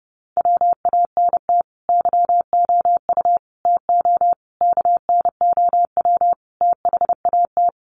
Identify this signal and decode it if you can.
Morse code, heard as a keyed tone, about 30 wpm, 700 Hz. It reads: WANT YOU TO KNOW THAT
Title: Sets of 5 Words Encore 30wpm